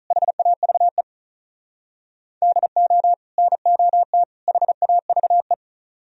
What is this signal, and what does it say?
Morse code, heard as a keyed tone, about 44 wpm, 700 Hz. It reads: HAVE DO NOT HAVE